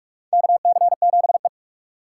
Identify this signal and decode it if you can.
Morse code, heard as a keyed tone, about 41 wpm, 700 Hz. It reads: KC7E
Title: USA Call Signs - 4x Character Spacing - 45wpm